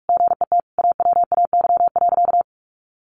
Morse code, heard as a keyed tone, about 44 wpm, 700 Hz. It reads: GET AWAY.
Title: Sentences from Top 600 Words 45wpm